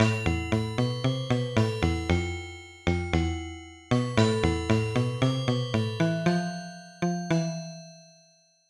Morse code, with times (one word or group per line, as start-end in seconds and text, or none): none